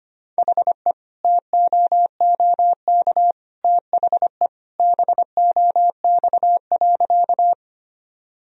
Morse code, heard as a keyed tone, about 25 wpm, 700 Hz. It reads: HE TOOK THE BOX.
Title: Sentences from Top 1000 Words - Review Part 2 - 25wpm